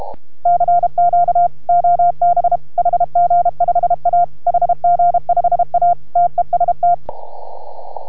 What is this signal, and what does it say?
Morse code, heard as a keyed tone, about 33 wpm, 700 Hz. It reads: CQ OB HG5A HG5A TEST